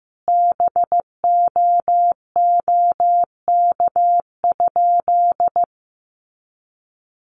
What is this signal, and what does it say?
Morse code, heard as a keyed tone, about 15 wpm, 700 Hz. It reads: BOOK?
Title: Sentences from Top 1200 Words 15wpm